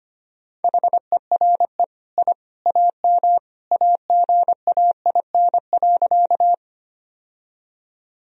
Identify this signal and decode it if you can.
Morse code, heard as a keyed tone, about 25 wpm, 700 Hz. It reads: HERE I AM AGAIN.
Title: Sentences from Top 1000 Words - Review Part 2 - 25wpm